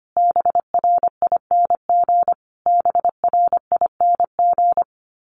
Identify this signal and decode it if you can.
Morse code, heard as a keyed tone, about 25 wpm, 700 Hz. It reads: BRING BRING
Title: Top 500 Words Repeated 25wpm